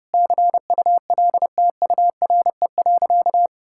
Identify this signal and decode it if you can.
Morse code, heard as a keyed tone, about 32 wpm, 700 Hz. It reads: CULTURE.
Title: Sentences from Top 1000 Words - Review Part 6 - 30wpm